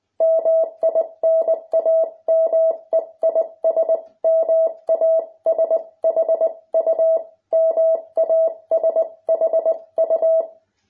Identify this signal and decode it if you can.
Morse code, heard as a keyed tone, about 19 wpm, 650 Hz. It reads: MI NAME IS MASHU MASHU